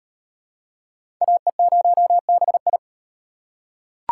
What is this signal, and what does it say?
Morse code, heard as a keyed tone, about 38 wpm, 700 Hz. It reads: AE0BI